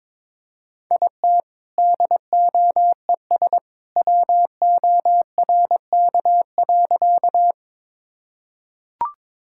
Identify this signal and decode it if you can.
Morse code, heard as a keyed tone, about 22 wpm, 700 Hz. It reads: IT DOES WORK.
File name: practice / 22wpm / Sentences with 2-Word Phrases - 22wpm